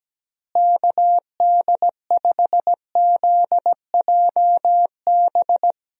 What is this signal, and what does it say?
Morse code, heard as a keyed tone, about 17 wpm, 700 Hz. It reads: KD5ZJB
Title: USA Call Signs - 6 Characters - Repeated 3x - 17wpm